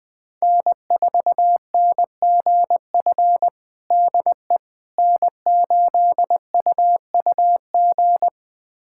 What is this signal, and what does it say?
Morse code, heard as a keyed tone, about 21 wpm, 700 Hz. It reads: N4NGF DE N8UUG
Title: USA Callsign DE Callsign - 5 Characters - 20wpm